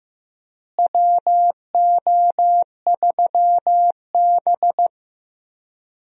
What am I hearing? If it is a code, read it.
Morse code, heard as a keyed tone, about 16 wpm, 700 Hz. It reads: WO3B
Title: USA Call Signs - 4 Characters - Repeated 3x - 15wpm